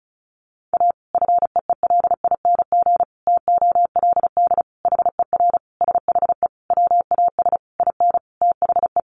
Morse code, heard as a keyed tone, about 34 wpm, 700 Hz. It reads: A FEELING TOLD HER SHE WAS IN THE
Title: Sentences from Top 1200 Words 35wpm